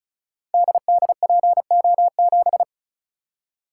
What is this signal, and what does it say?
Morse code, heard as a keyed tone, about 36 wpm, 700 Hz. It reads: DDPO7
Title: Code Groups - 5 Letter:Numbers - 35wpm